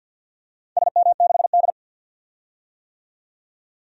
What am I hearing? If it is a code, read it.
Morse code, heard as a keyed tone, about 49 wpm, 700 Hz. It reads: SM6D
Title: DXCC-100 Call Signs - 4 Characters - 50wpm